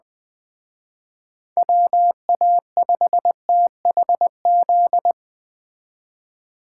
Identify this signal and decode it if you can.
Morse code, heard as a keyed tone, about 20 wpm, 700 Hz. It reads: WA5THZ